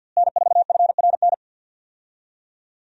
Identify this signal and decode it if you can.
Morse code, heard as a keyed tone, about 52 wpm, 700 Hz. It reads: N4FRN
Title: USA Call Signs - 5 Characters - Repeated 3x - 50wpm